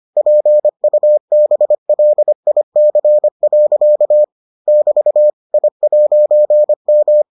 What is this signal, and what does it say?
Morse code, heard as a keyed tone, about 25 wpm, 600 Hz. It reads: PUBLIC. <BT> I'M